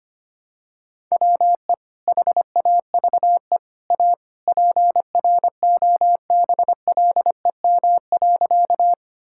Morse code, heard as a keyed tone, about 25 wpm, 700 Hz. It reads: WE HAVE A PROBLEM.
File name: practice / 25wpm / Sentences with 3-Word Phrases - 25wpm